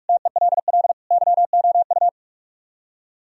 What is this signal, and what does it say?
Morse code, heard as a keyed tone, about 44 wpm, 700 Hz. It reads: TELL YOU